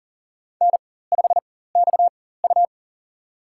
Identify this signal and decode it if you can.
Morse code, heard as a keyed tone, about 37 wpm, spaced as for 18 wpm, 700 Hz. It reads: N5XU
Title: USA Call Signs - 4 Characters - 4x Character Spacing - 40wpm